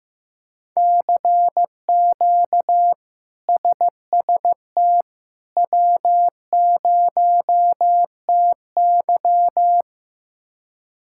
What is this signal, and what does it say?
Morse code, heard as a keyed tone, about 15 wpm, 700 Hz. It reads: CQ SST W0TY
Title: K1USN SST QSO Practice - 15wpm